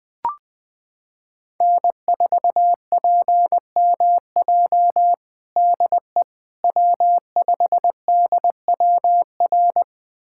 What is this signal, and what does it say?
Morse code, heard as a keyed tone, about 21 wpm, 700 Hz. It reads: N4PMJ DE W5DWR
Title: USA Callsign DE Callsign - 5 Characters - 20wpm